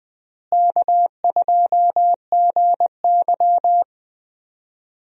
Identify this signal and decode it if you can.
Morse code, heard as a keyed tone, about 21 wpm, 700 Hz. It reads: K2GY
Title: USA Call Signs - 4 Characters - Repeated 2x - 20wpm